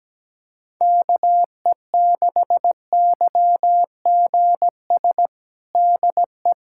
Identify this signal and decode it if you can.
Morse code, heard as a keyed tone, about 17 wpm, 700 Hz. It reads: KE6YGS DE
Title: USA Callsign DE Callsign 17wpm